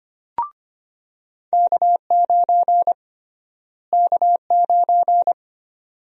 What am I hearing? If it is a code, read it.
Morse code, heard as a keyed tone, about 25 wpm, 700 Hz. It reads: K9 K9
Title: USA Call Sign Prefixes - Repeated 2x - 25wpm